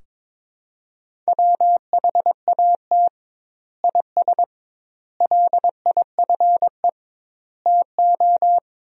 Morse code, heard as a keyed tone, about 21 wpm, 700 Hz. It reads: WHAT IS LIFE TO
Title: 2x Word-Spacing - Sentences from Top 300 Words - 22wpm